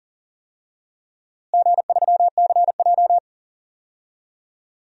Morse code, heard as a keyed tone, about 40 wpm, 700 Hz. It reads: G3CJ
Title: DXCC-100 Call Signs - 4 Characters - Repeated 3x - 40wpm